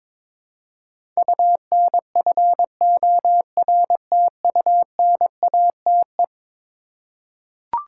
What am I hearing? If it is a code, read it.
Morse code, heard as a keyed tone, about 22 wpm, 700 Hz. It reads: UNFORTUNATE
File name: practice / 22wpm / Eleven Letter Words - 22wpm